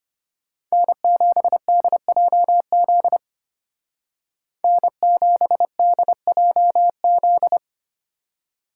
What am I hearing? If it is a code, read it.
Morse code, heard as a keyed tone, about 30 wpm, 700 Hz. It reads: N7DJZ N7DJZ
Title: Speed Racing - USA 1x3 Call Signs 25wpm